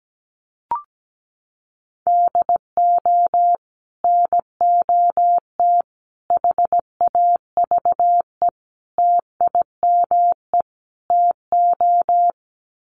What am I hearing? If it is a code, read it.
Morse code, heard as a keyed tone, about 17 wpm, 700 Hz. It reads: DO NOT HAVE TIME TO
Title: Sets of 5 Words 17wpm